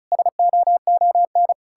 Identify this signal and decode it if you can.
Morse code, heard as a keyed tone, about 48 wpm, spaced as for 42 wpm, 700 Hz. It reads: SOON
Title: Speed-Racing Common QSO Phrases Encore - 35wpm